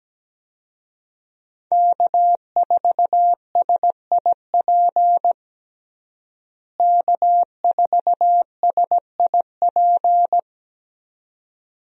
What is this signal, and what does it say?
Morse code, heard as a keyed tone, about 17 wpm, 700 Hz. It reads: K4SIP K4SIP
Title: USA Call Signs - 5 Characters - Repeated 2x - 17wpm